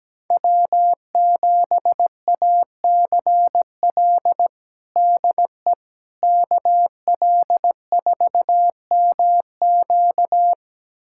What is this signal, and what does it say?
Morse code, heard as a keyed tone, about 17 wpm, 700 Hz. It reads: W7ACL DE KL4MQ
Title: USA Callsign DE Callsign - 5 Characters - 17wpm